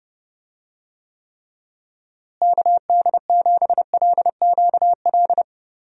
Morse code, heard as a keyed tone, about 30 wpm, 700 Hz. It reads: KD7LQL
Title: USA Call Signs - 6 Characters - Repeated 2x - 30wpm